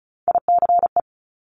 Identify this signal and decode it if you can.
Morse code, heard as a keyed tone, about 36 wpm, 700 Hz. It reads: ICE
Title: Multi-letter Phonemes 35wpm